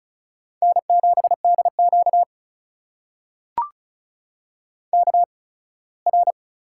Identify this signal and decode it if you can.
Morse code, heard as a keyed tone, about 36 wpm, spaced as for 21 wpm, 700 Hz. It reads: N7DQ KR
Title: USA Call Signs - 8x Character Spacing - 35wpm